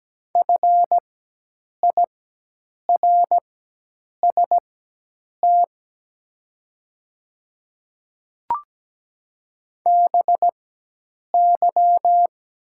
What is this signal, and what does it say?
Morse code, heard as a keyed tone, about 17 wpm, spaced as for 8 wpm, 700 Hz. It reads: FIRST BY
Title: 4x Character Spacing - Top 500 Words - 17wpm